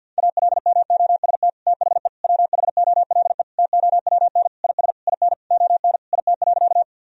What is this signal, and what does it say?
Morse code, heard as a keyed tone, about 50 wpm, 700 Hz. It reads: ALMOST THE WHOLE TOWN IS IN ON IT.
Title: Sentences from Top 1000 Words - Review Part 4 - 50wpm